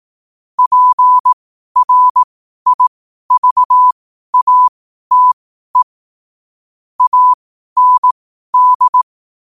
Morse code, heard as a keyed tone, about 18 wpm, spaced as for 12 wpm, 1000 Hz. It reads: PRIVATE AND
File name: qotdfiles / 12wpm